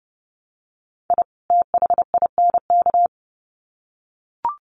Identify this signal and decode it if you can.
Morse code, heard as a keyed tone, about 28 wpm, 700 Hz. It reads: I THINK
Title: Sets of 2 Words 30wpm